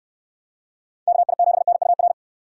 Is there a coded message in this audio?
Yes: Morse code from 1.1 to 2.1 s, heard as a keyed tone, about 69 wpm, 700 Hz. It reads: KI7NUR